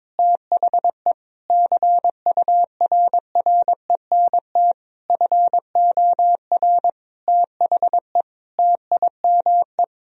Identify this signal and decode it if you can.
Morse code, heard as a keyed tone, about 22 wpm, 700 Hz. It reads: THE CURRENT FOR THE TIME
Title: Sentences from Top 1000 Words - Review Part 4 - 22wpm